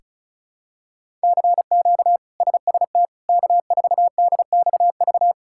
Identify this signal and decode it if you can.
Morse code, heard as a keyed tone, about 36 wpm, 700 Hz. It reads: CQ SST K4DXV
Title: K1USN SST QSO Practice - 35wpm